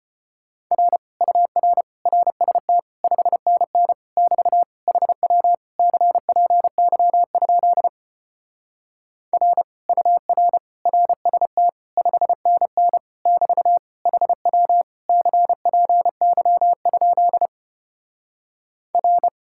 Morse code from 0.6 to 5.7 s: R UR RST 5NN <BT> HW

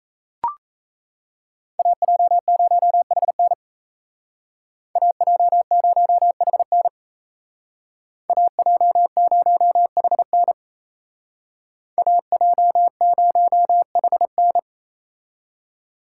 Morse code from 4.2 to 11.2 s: AJ0HN AJ0HN

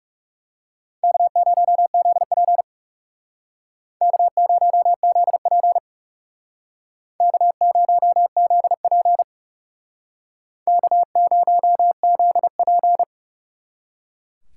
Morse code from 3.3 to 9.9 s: K0ZP K0ZP

